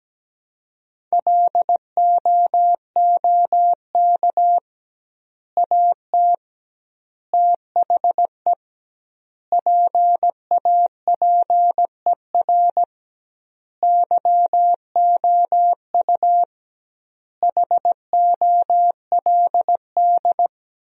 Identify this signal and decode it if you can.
Morse code, heard as a keyed tone, about 17 wpm, 700 Hz. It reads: LOOK AT THE PAPER YOU HOLD
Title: 2x Word-Spacing - Sentences from Top 400 Words - 17wpm